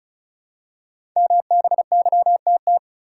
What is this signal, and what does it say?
Morse code, heard as a keyed tone, about 36 wpm, 700 Hz. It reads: MBYTT
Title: Code Groups - 5 Letter:Numbers - 35wpm